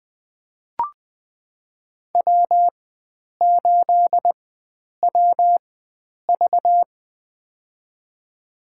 Morse code, heard as a keyed tone, about 21 wpm, spaced as for 9 wpm, 700 Hz. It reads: W8WV
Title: USA Call Signs - 4 Characters - 4x Character Spacing - 20wpm